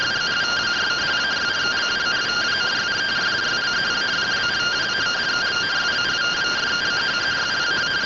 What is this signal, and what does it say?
Two-tone teleprinter signal, about 170 Hz shift.